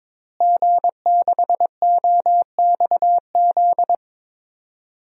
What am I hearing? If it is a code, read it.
Morse code, heard as a keyed tone, about 22 wpm, 700 Hz. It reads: G6OXZ
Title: DXCC-100 Call Signs - 5 Characters - Repeated 3x - 22wpm